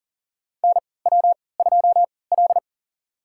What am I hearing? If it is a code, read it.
Morse code, heard as a keyed tone, about 40 wpm, spaced as for 23 wpm, 700 Hz. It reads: NW2L
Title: USA Call Signs - 3x Character Spacing - 40wpm